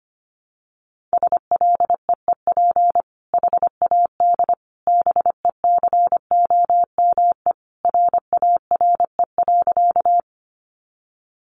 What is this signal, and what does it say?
Morse code, heard as a keyed tone, about 25 wpm, 700 Hz. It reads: SLEEP HAD BECOME RARE.
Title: Sentences from Top 1100 Words 25wpm